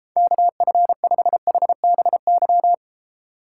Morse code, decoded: KF5HBY

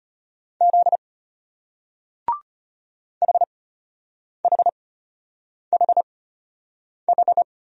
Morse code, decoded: Z HHHH

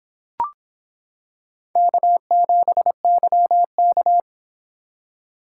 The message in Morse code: K7YK

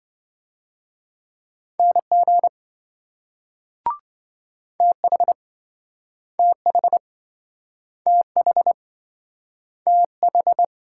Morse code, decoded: NG TH TH TH TH

